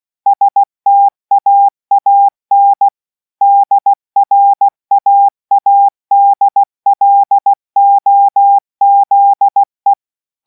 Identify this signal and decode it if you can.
Morse code, heard as a keyed tone, about 16 wpm, 800 Hz. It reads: STAAN DRAADLOZE